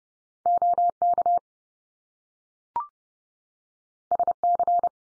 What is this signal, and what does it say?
Morse code, heard as a keyed tone, about 30 wpm, 700 Hz. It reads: OK SC
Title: US States 30WPM